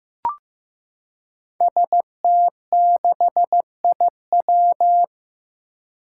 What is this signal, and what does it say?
Morse code, heard as a keyed tone, about 15 wpm, 700 Hz. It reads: ST6IW
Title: Five Challenging Letter-Number Combinations 15wpm